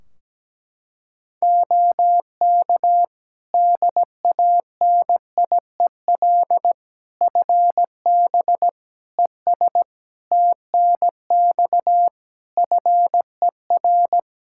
Morse code, heard as a keyed tone, about 17 wpm, 700 Hz. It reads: OK DANIEL FB ES TNX FER
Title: Practice QSO - Exchange 2 - Info - 17wpm